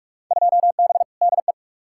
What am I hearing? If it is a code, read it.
Morse code, heard as a keyed tone, about 46 wpm, 700 Hz. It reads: 2B DE
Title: Field Day Exchanges Encore - 45wpm